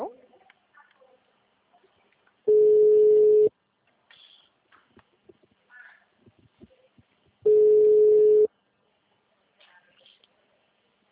Ringback tone (Europe, 425 Hz).
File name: out- -2011- - - .1979